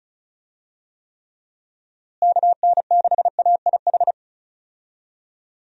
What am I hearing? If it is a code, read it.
Morse code, heard as a keyed tone, about 34 wpm, 700 Hz. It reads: KN6AIH
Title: USA Call Signs - 6 Characters - Repeated 2x - 35wpm